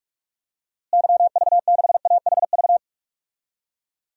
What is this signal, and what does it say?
Morse code, heard as a keyed tone, about 41 wpm, 700 Hz. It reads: YV6AHV
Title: DXCC-100 Call Signs - 6 Characters - Repeated 2x - 45wpm